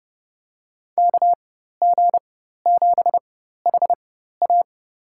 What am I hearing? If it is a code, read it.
Morse code, heard as a keyed tone, about 30 wpm, spaced as for 14 wpm, 700 Hz. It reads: KG7HA
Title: USA Call Signs - 5 Characters - 4x Character Spacing - 30wpm